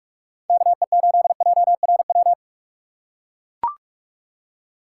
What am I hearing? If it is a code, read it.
Morse code, heard as a keyed tone, about 43 wpm, 700 Hz. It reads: KE8JRW
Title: USA Call Signs - 6 Characters - 4x Character Spacing - 45wpm